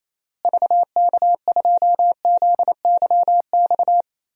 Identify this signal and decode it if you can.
Morse code, heard as a keyed tone, about 27 wpm, 700 Hz. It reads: VK2ZYX
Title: DXCC-100 Call Signs - 6 Characters - Repeated 3x - 28wpm